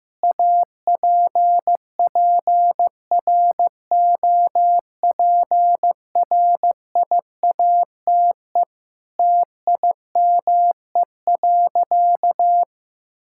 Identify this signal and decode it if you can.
Morse code, heard as a keyed tone, about 15 wpm, 700 Hz. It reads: APPROPRIATE TIME.
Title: Sentences from Top 1000 Words - Review Part 5 - 15wpm